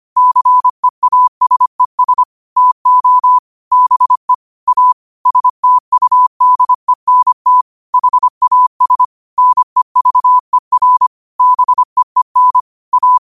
Morse code, heard as a keyed tone, about 25 wpm, 1000 Hz. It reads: CEASES TO BE A STUDENT HAS NEVER BEEN A
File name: qotdfiles / 25wpm